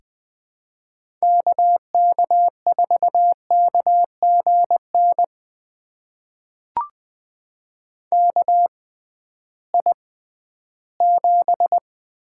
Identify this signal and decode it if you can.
Morse code, heard as a keyed tone, about 20 wpm, spaced as for 13 wpm, 700 Hz. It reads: KK4KGN KI7
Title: USA Call Signs - 6 Characters - 6x Character Spacing - 20wpm